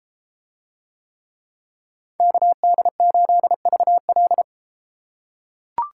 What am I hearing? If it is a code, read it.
Morse code, heard as a keyed tone, about 33 wpm, 700 Hz. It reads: KD8VL